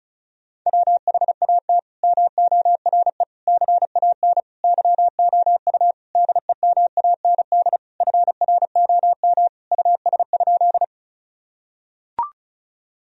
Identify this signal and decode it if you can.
Morse code, heard as a keyed tone, about 36 wpm, 700 Hz. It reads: WHAT MORE CAN YOU DEMAND FROM US?
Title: Sentences from Top 1000 Words - Review Part 4 - 35wpm